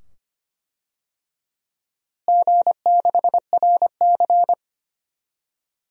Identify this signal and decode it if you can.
Morse code, heard as a keyed tone, about 25 wpm, 700 Hz. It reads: G6RC
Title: DXCC-100 Call Signs - 4 Characters - Repeated 2x - 25wpm